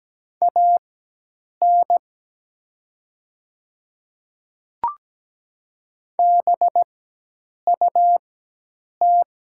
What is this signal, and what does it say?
Morse code, heard as a keyed tone, about 17 wpm, spaced as for 8 wpm, 700 Hz. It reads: AN BUT
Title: 4x Character Spacing - Top 100 Words - 17wpm